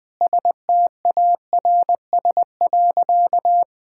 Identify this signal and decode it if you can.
Morse code, heard as a keyed tone, about 21 wpm, 700 Hz. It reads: STARS.
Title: Sentences with 2-Word Phrases - Spoken Beginning - 20wpm